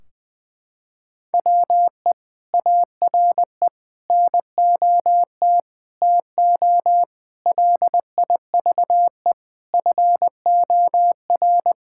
Morse code, heard as a keyed tone, about 20 wpm, 700 Hz. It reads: WE ARE NOT TO LIVE FOR
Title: Sentences from Top 1600 Words 20wpm